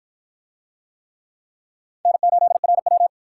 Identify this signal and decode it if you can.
Morse code, heard as a keyed tone, about 50 wpm, 700 Hz. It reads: N8RW